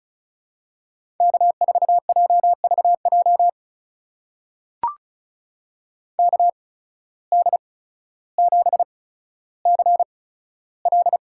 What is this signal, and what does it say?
Morse code, heard as a keyed tone, about 35 wpm, spaced as for 15 wpm, 690 Hz. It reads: K4JVJ KD7CL
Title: USA Call Signs - 5 Characters - 8x Character Spacing - 35wpm